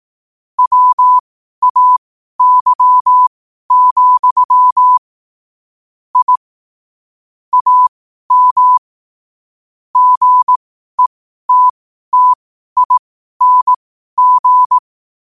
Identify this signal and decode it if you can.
Morse code, heard as a keyed tone, about 18 wpm, spaced as for 12 wpm, 1000 Hz. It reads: WAY, I AM GETTING